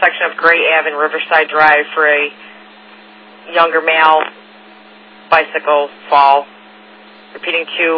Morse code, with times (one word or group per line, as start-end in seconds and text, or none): none